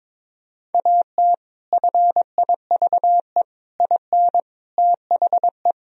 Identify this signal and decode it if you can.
Morse code, heard as a keyed tone, about 22 wpm, 700 Hz. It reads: AT FIVE IN THE
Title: Sentences from Top 1000 Words - Review Part 2 - 22wpm